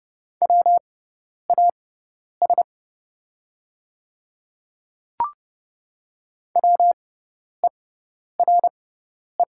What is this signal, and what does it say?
Morse code, heard as a keyed tone, about 30 wpm, spaced as for 10 wpm, 700 Hz. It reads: WAS WERE
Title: 6x Character Spacing - Top 100 Words - 30wpm